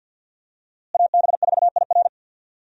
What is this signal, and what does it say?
Morse code, heard as a keyed tone, about 51 wpm, 700 Hz. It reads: AB4IR
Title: USA Call Signs - 5 Characters - Repeated 2x - 50wpm